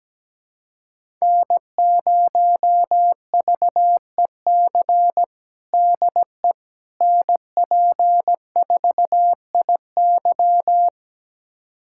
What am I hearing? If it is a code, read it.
Morse code, heard as a keyed tone, about 17 wpm, 700 Hz. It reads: N0VEC DE NP4IY
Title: USA Callsign DE Callsign - 5 Characters - 17wpm